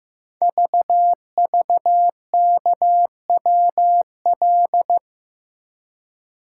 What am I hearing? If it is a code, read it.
Morse code, heard as a keyed tone, about 15 wpm, 700 Hz. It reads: VVKWL
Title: Code Groups - 5 Letters - 15wpm